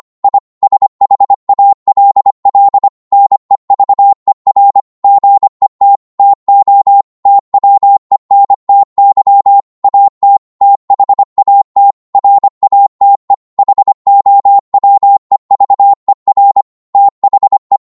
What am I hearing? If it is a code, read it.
Morse code, heard as a keyed tone, about 25 wpm, 800 Hz. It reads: I SHALL NEVER GET TO TWENTY AT THAT RATE HOWEVER THE